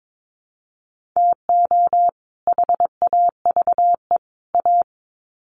Morse code, heard as a keyed tone, about 22 wpm, 700 Hz. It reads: TO HAVE A
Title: Sets of 3 Words 22wpm